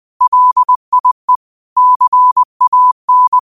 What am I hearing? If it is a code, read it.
Morse code, heard as a keyed tone, about 20 wpm, 1000 Hz. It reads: LIE CAN
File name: qotdfiles / 20wpm